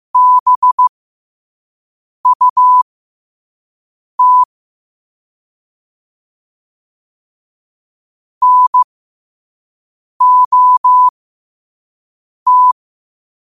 Morse code, heard as a keyed tone, about 15 wpm, spaced as for 5 wpm, 1000 Hz. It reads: BUT NOT
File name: qotdfiles / 5wpm